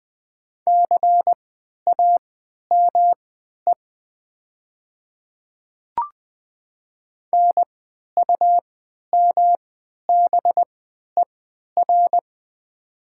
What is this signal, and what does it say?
Morse code, heard as a keyed tone, about 20 wpm, spaced as for 11 wpm, 700 Hz. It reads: CAME NUMBER
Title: 3x Character Spacing - Top 200 Words - 20wpm